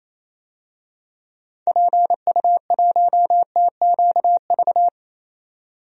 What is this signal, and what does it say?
Morse code, heard as a keyed tone, about 28 wpm, 700 Hz. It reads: PU1TQV
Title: DXCC-100 Call Signs - 6 Characters - 28wpm